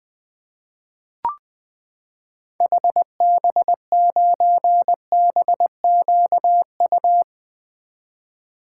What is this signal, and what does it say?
Morse code, heard as a keyed tone, about 20 wpm, 700 Hz. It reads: HB9BQU
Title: 2023 SOTA Activator Call Signs - 20wpm